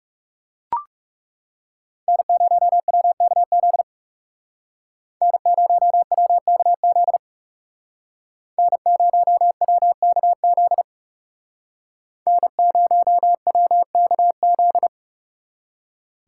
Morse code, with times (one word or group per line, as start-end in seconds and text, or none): 2.1-3.8 N0WKZ
5.2-7.2 N0WKZ
8.6-10.8 N0WKZ
12.3-14.9 N0WKZ